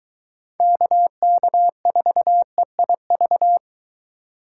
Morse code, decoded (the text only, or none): KK4EIV